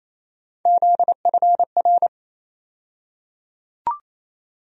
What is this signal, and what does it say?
Morse code, heard as a keyed tone, about 28 wpm, 700 Hz. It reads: ZFR
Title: USA Call Sign Postfixes - 3 Letters 28wpm